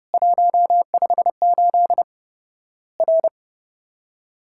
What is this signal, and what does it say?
Morse code, heard as a keyed tone, about 30 wpm, 700 Hz. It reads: 158 R
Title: Worked All Europe DX Contest - QTC Exchange - 30wpm